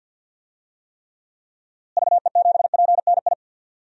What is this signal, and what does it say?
Morse code, heard as a keyed tone, about 52 wpm, 700 Hz. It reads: VE7PNI
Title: DXCC-100 Call Signs - 6 Characters - Repeated 3x - 50wpm